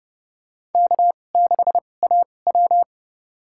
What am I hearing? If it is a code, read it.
Morse code, heard as a keyed tone, about 30 wpm, spaced as for 22 wpm, 700 Hz. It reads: K6AW
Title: USA Call Signs - 4 Characters - 2x Character Spacing - 30wpm